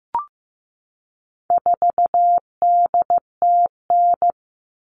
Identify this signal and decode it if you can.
Morse code, heard as a keyed tone, about 15 wpm, 700 Hz. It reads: 4DTN
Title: All Cut Numbers - 4-Digit 15wpm